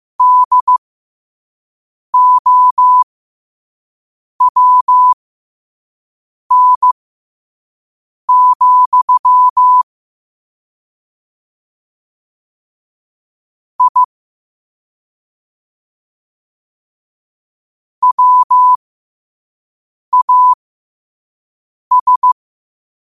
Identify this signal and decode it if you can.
Morse code, heard as a keyed tone, about 15 wpm, spaced as for 5 wpm, 1000 Hz. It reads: DOWN, I WAS
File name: qotdfiles / 5wpm